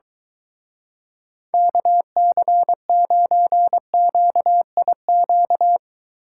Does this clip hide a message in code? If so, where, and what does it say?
Yes: Morse code from 1.5 to 5.8 s, heard as a keyed tone, about 22 wpm, 700 Hz. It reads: KC9QIQ